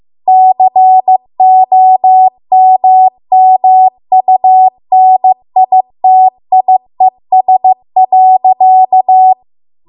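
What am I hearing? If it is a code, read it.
Morse code, heard as a keyed tone, about 15 wpm, 750 Hz. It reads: COMMUNITIES.
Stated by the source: Morse code practice at 15 wpm from the national amateur radio association's code archive.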